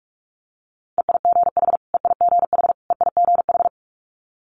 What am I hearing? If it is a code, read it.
Morse code, heard as a keyed tone, about 46 wpm, 700 Hz. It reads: EIGH EIGH EIGH
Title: Multi-letter Phonemes Repeated 3x 45wpm